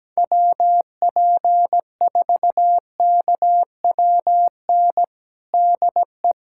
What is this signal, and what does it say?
Morse code, heard as a keyed tone, about 17 wpm, 700 Hz. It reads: WP4KWN DE
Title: USA Callsign DE Callsign - 6 Characters - 17wpm